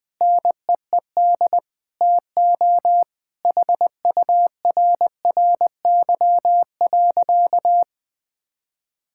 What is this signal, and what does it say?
Morse code, heard as a keyed tone, about 20 wpm, 700 Hz. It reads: NEED TO HURRY.
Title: Sentences with 4-Word Phrases - 20wpm